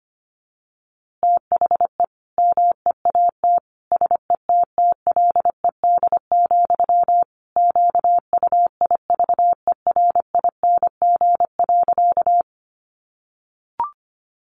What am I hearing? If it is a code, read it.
Morse code, heard as a keyed tone, about 25 wpm, 700 Hz. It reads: THE MEAT SETTLED, QUIVERING.
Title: A Sound Of Thunder 25wpm